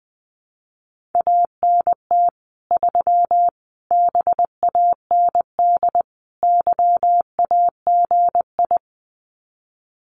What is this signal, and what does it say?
Morse code, heard as a keyed tone, about 20 wpm, 700 Hz. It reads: ANT 3 BAND YAGI
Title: Common QSO Phrases 20wpm